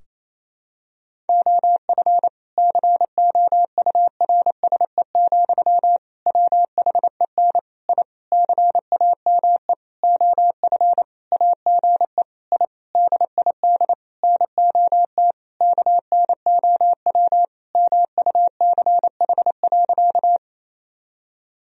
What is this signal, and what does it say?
Morse code, heard as a keyed tone, about 28 wpm, 700 Hz. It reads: OF COURSE, WHEN I CAME OF AGE I DID NOT KNOW MUCH.
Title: Sentences from Top 1000 Words - Review Part 3 - 28wpm